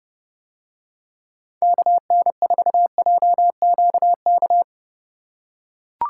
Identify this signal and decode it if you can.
Morse code, heard as a keyed tone, about 29 wpm, 700 Hz. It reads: KN4JQK